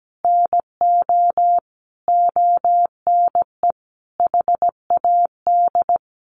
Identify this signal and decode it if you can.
Morse code, heard as a keyed tone, about 17 wpm, 700 Hz. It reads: NO ONE HAD
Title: Sentences from Top 1200 Words 17wpm